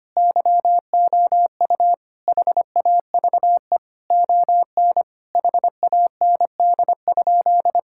Morse code, heard as a keyed tone, about 25 wpm, 700 Hz. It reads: YOU HAVE ON HAND?
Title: Sentences from Top 1000 Words - Review Part 5 - 25wpm